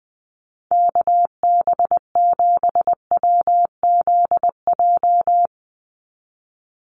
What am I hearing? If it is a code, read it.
Morse code, heard as a keyed tone, about 19 wpm, 700 Hz. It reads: KB7WZJ
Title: USA Call Signs 20wpm